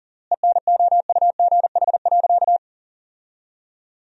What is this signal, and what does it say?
Morse code, heard as a keyed tone, about 40 wpm, 700 Hz. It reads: ENOUGH.
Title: 3x Word-Spacing - Sentences from Top 400 Words - 40wpm